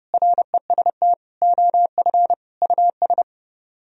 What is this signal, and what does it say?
Morse code, heard as a keyed tone, about 30 wpm, 700 Hz. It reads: REST OF US